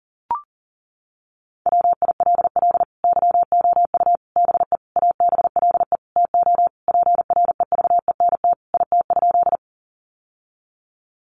Morse code, heard as a keyed tone, about 40 wpm, 700 Hz. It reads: WILL YOU BE ABLE TO PREVENT IT?
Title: Sentences from Top 1100 Words 40wpm